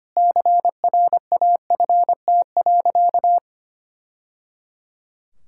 Morse code, heard as a keyed tone, about 25 wpm, 700 Hz. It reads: CRAFT.
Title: Sentences from Top 1000 Words - Review Part 6 - 25wpm